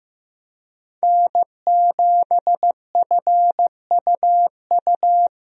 Morse code, heard as a keyed tone, about 16 wpm, 700 Hz. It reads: N7FUU
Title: USA Call Signs - 5 Characters - Repeated 3x - 15wpm